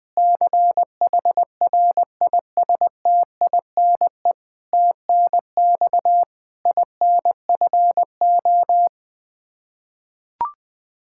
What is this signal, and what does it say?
Morse code, heard as a keyed tone, about 21 wpm, 700 Hz. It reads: CHRISTINE TNX INFO
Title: Practice QSO - Exchange 3 - More Info - 20wpm